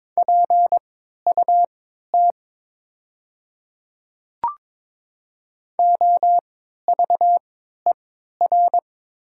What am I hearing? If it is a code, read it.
Morse code, heard as a keyed tone, about 22 wpm, spaced as for 13 wpm, 700 Hz. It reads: PUT OVER
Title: 3x Character Spacing - Top 200 Words - 22wpm